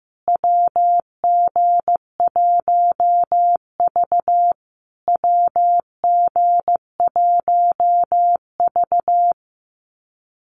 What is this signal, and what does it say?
Morse code, heard as a keyed tone, about 15 wpm, 700 Hz. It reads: WG1V WG1V
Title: USA Call Signs Repeated 3x 15wpm